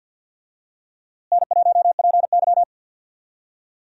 Morse code, heard as a keyed tone, about 49 wpm, 700 Hz. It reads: N1PY
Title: USA Call Signs - 4 Characters - Repeated 2x - 50wpm